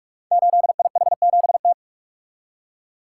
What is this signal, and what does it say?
Morse code, heard as a keyed tone, about 44 wpm, 700 Hz. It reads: 8IH7T
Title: Code Groups - 5 Letter:Numbers - 45wpm